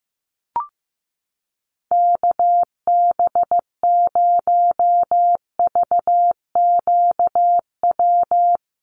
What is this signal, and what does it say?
Morse code, heard as a keyed tone, about 15 wpm, 700 Hz. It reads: KB0VQW
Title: USA Call Signs Repeated 2x 15wpm